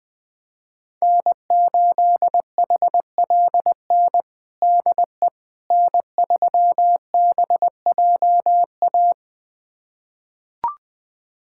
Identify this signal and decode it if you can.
Morse code, heard as a keyed tone, about 21 wpm, 700 Hz. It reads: N8HLN DE N3BJA
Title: USA Callsign DE Callsign - 5 Characters - 20wpm